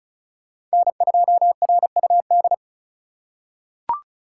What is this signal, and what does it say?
Morse code, heard as a keyed tone, about 36 wpm, 700 Hz. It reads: N2RUD
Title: USA Call Signs - 5 Characters - 3x Character Spacing - 35wpm